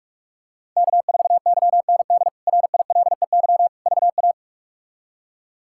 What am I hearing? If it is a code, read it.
Morse code, heard as a keyed tone, about 47 wpm, 700 Hz. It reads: K4YND RILEY VA
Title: K1USN SST QSO Practice - 45wpm